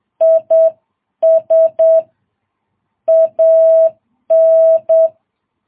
Morse code, heard as a keyed tone, about 8 wpm, 650 Hz. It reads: IS AN
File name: jh8cbh